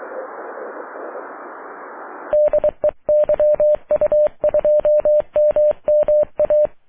Morse code, heard as a keyed tone, about 23 wpm, 600 Hz. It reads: DE YU2MMA
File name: wp / la8aja yu2mma aurora 05 aug 2011